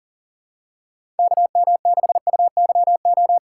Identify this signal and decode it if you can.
Morse code, heard as a keyed tone, about 39 wpm, 700 Hz. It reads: KM6UYO
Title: USA Call Signs - 6 Characters - 3x Character Spacing - 40wpm